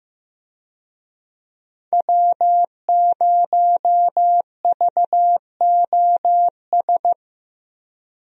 Morse code, heard as a keyed tone, about 15 wpm, 700 Hz. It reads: W0VOS
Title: USA Call Signs - 5 Characters - Repeated 3x - 15wpm